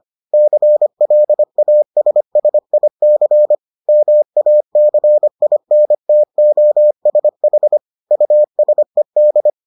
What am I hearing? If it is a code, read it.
Morse code, heard as a keyed tone, about 25 wpm, 600 Hz. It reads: CLASSIC MACINTOSH USED